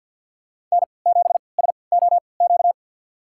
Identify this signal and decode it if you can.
Morse code, heard as a keyed tone, about 46 wpm, spaced as for 27 wpm, 700 Hz. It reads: N7SOQ
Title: USA Call Signs - 5 Characters - 3x Character Spacing - 50wpm